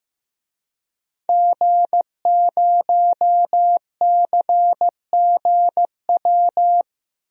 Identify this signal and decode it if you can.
Morse code, heard as a keyed tone, about 15 wpm, 700 Hz. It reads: G0CGW
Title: DXCC-100 Call Signs - 5 Characters - 15wpm